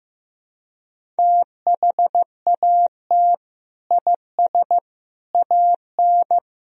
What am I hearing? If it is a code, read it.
Morse code, heard as a keyed tone, about 15 wpm, 700 Hz. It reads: THAT IS AN
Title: Sentences from Top 1000 Words - Review Part 1 - 15wpm